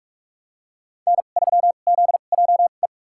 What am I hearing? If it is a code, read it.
Morse code, heard as a keyed tone, about 46 wpm, spaced as for 33 wpm, 700 Hz. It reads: N3ZJE
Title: USA Call Signs - 5 Characters - 2x Character Spacing - 45wpm